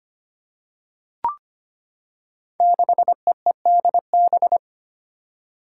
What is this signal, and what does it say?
Morse code, heard as a keyed tone, about 25 wpm, 700 Hz. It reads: 6EEDB
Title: Five Challenging Letter-Number Combinations 25wpm